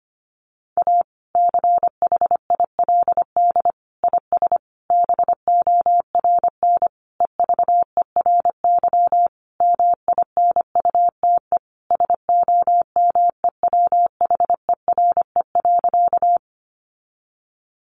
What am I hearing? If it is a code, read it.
Morse code, heard as a keyed tone, about 25 wpm, 700 Hz. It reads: A CHILD IS BORN EVERY MINUTE SOMEWHERE.
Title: Sentences from Top 1400 Words 25wpm